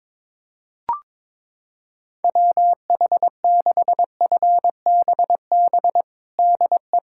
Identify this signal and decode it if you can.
Morse code, heard as a keyed tone, about 22 wpm, 700 Hz. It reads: WH6FBB DE
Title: USA Callsign DE Callsign 22wpm